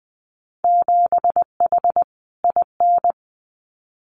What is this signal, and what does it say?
Morse code, heard as a keyed tone, about 20 wpm, 700 Hz. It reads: 7H IN
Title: Winter Field Day Exchanges Encore 20wpm